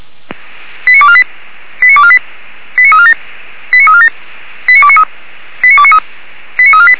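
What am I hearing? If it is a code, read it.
Five-tone selective call (ZVEI), heard as quick runs of five tones, tones of about 70 ms. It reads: 89248 89248 89347 89347 89293 89293 89348
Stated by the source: SDR reception of ZVEI five-tone call 89332.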